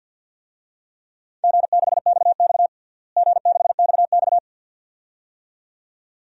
Morse code, heard as a keyed tone, about 52 wpm, 700 Hz. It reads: G6XX G6XX
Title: DXCC-100 Call Signs - 4 Characters - Repeated 2x - 50wpm